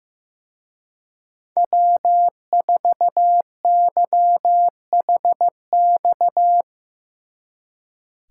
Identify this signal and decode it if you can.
Morse code, heard as a keyed tone, about 17 wpm, 700 Hz. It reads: W4YHX
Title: Speed Racing - USA Call Signs 15wpm